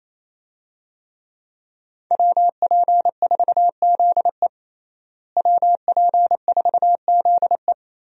Morse code, heard as a keyed tone, about 28 wpm, 700 Hz. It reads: WP4ZE WP4ZE
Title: USA Call Signs - 5 Characters - Repeated 3x - 28wpm